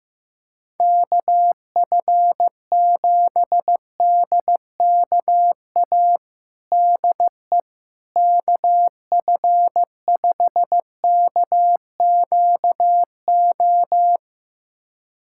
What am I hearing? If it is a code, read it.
Morse code, heard as a keyed tone, about 15 wpm, 700 Hz. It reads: KF7DKA DE KF5KQO
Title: USA Callsign DE Callsign - 6 Characters - 15wpm